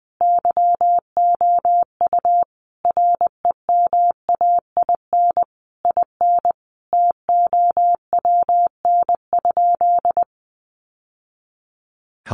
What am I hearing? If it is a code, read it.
Morse code, heard as a keyed tone, about 19 wpm, 700 Hz. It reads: YOU REMAIN IN TOWN?
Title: Sentences from Top 1200 Words 20wpm